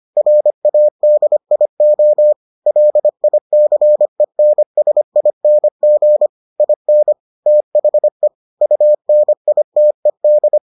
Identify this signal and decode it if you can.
Morse code, heard as a keyed tone, about 25 wpm, 600 Hz. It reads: RADIO LICENSING IN THE UNITED